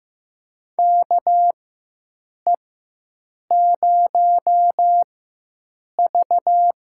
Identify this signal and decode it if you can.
Morse code, heard as a keyed tone, about 15 wpm, spaced as for 7 wpm, 700 Hz. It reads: KE0V
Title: USA Call Signs - 4x Character Spacing - 15wpm